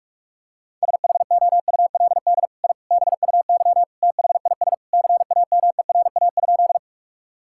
Morse code, heard as a keyed tone, about 46 wpm, 700 Hz. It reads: SHOULD I BUY THIS CAMERA?
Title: Sentences from Top 1000 Words - Review Part 5 - 45wpm